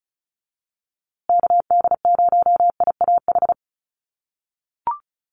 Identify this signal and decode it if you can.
Morse code, heard as a keyed tone, about 34 wpm, 700 Hz. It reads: KD0IAH
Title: USA Call Signs Repeated 2x 35wpm